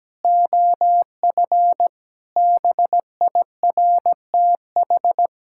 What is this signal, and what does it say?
Morse code, heard as a keyed tone, about 17 wpm, 700 Hz. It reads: OF BIRTH
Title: Sentences from Top 1000 Words - Review Part 5 - 17wpm